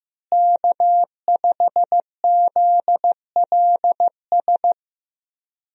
Morse code, heard as a keyed tone, about 15 wpm, 700 Hz. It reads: K5ZLS
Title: USA Call Signs - 5 Characters - Repeated 3x - 15wpm